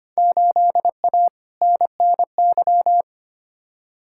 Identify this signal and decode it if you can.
Morse code, heard as a keyed tone, about 25 wpm, 700 Hz. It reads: 8A NNY
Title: Field Day Exchanges Encore - 25wpm